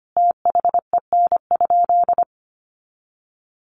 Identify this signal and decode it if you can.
Morse code, heard as a keyed tone, about 25 wpm, 700 Hz. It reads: THEN?
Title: Sentences from Top 100 Words 25wpm